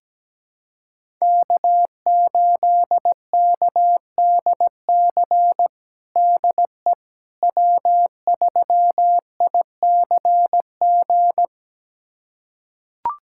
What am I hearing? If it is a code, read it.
Morse code, heard as a keyed tone, about 17 wpm, 700 Hz. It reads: K8KDC DE W3ICG
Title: USA Callsign DE Callsign - 5 Characters - 17wpm